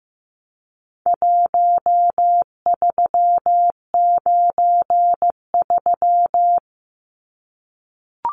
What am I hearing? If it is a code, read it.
Morse code, heard as a keyed tone, about 15 wpm, 700 Hz. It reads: 1393
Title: Numbers - 4-Digit 15wpm